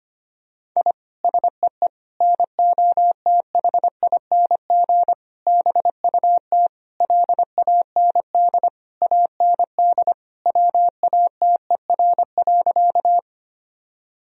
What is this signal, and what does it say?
Morse code, heard as a keyed tone, about 25 wpm, 700 Hz. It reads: I SEE NOTHING BUT LAND AND WATER.
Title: Sentences from Top 1000 Words - Review Part 1 - 25wpm